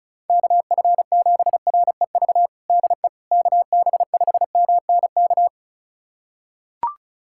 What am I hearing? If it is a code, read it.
Morse code, heard as a keyed tone, about 36 wpm, 700 Hz. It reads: KF7REV DE KB5MNK
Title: USA Callsign DE Callsign - 6 Characters - 35wpm